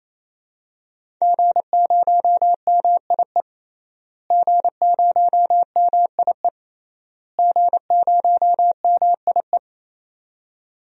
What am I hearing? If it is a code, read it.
Morse code, heard as a keyed tone, about 28 wpm, 700 Hz. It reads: G0MIE G0MIE G0MIE
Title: DXCC-100 Call Signs - 5 Characters - Repeated 3x - 28wpm